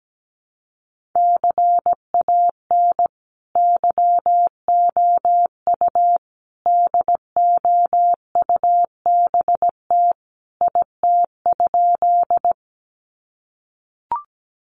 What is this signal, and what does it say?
Morse code, heard as a keyed tone, about 17 wpm, 700 Hz. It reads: CAN YOU DOUBT IT?
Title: Sentences from Top 1500 Words 17wpm